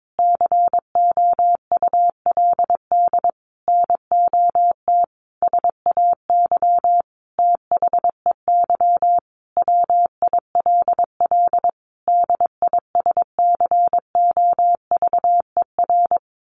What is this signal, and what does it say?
Morse code, heard as a keyed tone, about 22 wpm, 700 Hz. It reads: COULD NOT SAY THEY WILL DISCOVER
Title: Sentences from Top 1100 Words 22wpm